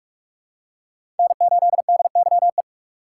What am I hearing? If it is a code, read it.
Morse code, heard as a keyed tone, about 42 wpm, 690 Hz. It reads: N8DYE
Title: USA Call Signs - 5 Characters - 6x Character Spacing - 45wpm